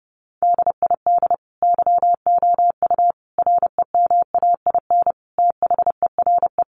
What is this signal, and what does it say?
Morse code, heard as a keyed tone, about 30 wpm, 700 Hz. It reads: DID YOU REMAIN THERE
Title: Sentences from Top 1200 Words 30wpm